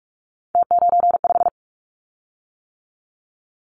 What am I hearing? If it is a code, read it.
Morse code, heard as a keyed tone, about 44 wpm, 700 Hz. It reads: T95
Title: Callsign Prefixes for Top 100 DXCC - 45wpm